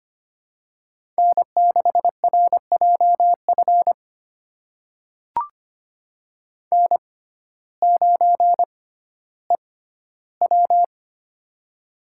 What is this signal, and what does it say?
Morse code, heard as a keyed tone, about 25 wpm, spaced as for 14 wpm, 700 Hz. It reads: N6RJF N9EW
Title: USA Call Signs - 6x Character Spacing - 25wpm